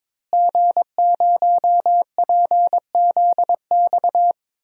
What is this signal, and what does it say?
Morse code, heard as a keyed tone, about 22 wpm, 700 Hz. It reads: G0PZX
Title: DXCC-100 Call Signs - 5 Characters - Repeated 3x - 22wpm